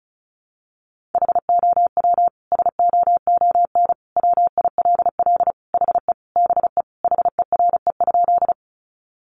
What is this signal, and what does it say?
Morse code, heard as a keyed tone, about 34 wpm, 700 Hz. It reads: HOW SOON WILL HE BE HERE?
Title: Sentences from Top 400 Words 35wpm